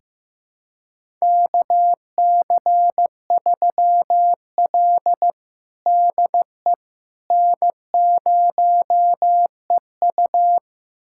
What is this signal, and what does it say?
Morse code, heard as a keyed tone, about 15 wpm, 700 Hz. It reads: KC3L DE N0EU
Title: USA Callsign DE Callsign - 4 Characters - 15wpm